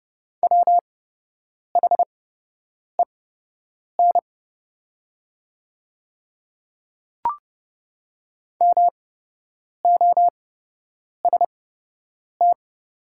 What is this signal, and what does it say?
Morse code, heard as a keyed tone, about 30 wpm, spaced as for 8 wpm, 700 Hz. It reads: WHEN MOST
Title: 8x Character Spacing - Top 100 Words - 30wpm